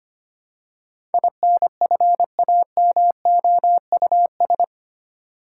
Morse code, heard as a keyed tone, about 25 wpm, 700 Hz. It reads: INFAMOUS
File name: Eight Letter Words - 25wpm